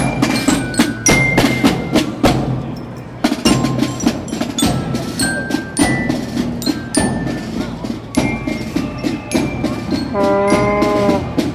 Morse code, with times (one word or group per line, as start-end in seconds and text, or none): none